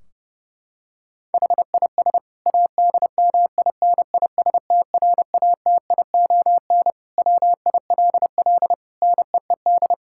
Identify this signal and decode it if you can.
Morse code, heard as a keyed tone, about 30 wpm, 700 Hz. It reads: HIS ADMINISTRATION WILL NEED
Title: Sentences from Top 1600 Words 30wpm